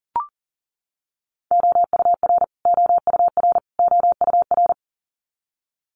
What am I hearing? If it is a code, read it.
Morse code, heard as a keyed tone, about 40 wpm, 700 Hz. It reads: OUR OUR OUR
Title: Top 500 Words Repeated 40wpm